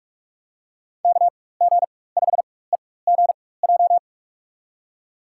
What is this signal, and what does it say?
Morse code, heard as a keyed tone, about 42 wpm, spaced as for 21 wpm, 700 Hz. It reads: KG5EGJ
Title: USA Call Signs - 4x Character Spacing - 45wpm